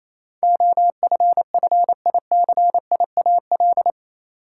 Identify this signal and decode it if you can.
Morse code, heard as a keyed tone, about 28 wpm, 700 Hz. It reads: OFFICIAL
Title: Sentences from Top 1000 Words - New Word Drill - 28wpm